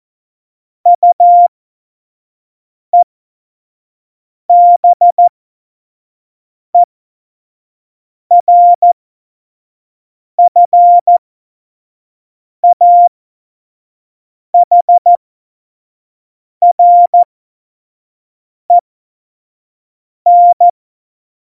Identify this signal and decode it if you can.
Morse code, heard as a keyed tone, about 14 wpm, spaced as for 5 wpm, 700 Hz. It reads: UEBERFAHREN
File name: Morsekurs / Witze / Witz 13 March 2018, 25 F70